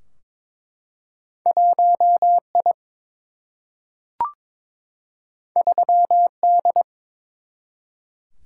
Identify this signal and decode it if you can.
Morse code, heard as a keyed tone, about 22 wpm, 700 Hz. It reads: 1 I 3 D
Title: Single-Letter Single-Number 22wpm